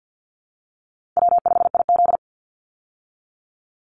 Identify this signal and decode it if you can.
Morse code, heard as a keyed tone, about 48 wpm, 700 Hz. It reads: W5IZ
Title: Easy Call Signs Repeated 3x - 50wpm